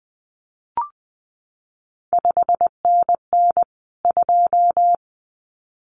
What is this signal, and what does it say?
Morse code, heard as a keyed tone, about 20 wpm, 700 Hz. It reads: 5NN 2
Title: CQ WW DX Contest Exchange - 20wpm